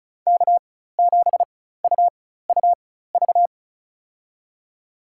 Morse code, heard as a keyed tone, about 36 wpm, spaced as for 16 wpm, 700 Hz. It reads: K7UUV
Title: USA Call Signs - 5 Characters - 4x Character Spacing - 35wpm